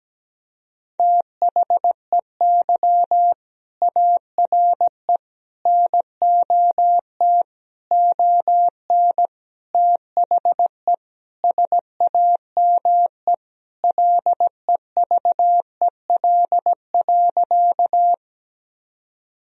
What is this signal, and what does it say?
Morse code, heard as a keyed tone, about 17 wpm, 700 Hz. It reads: THEY ARE NOT ON THE SAME LEVEL.
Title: Sentences from Top 1000 Words - Review Part 4 - 17wpm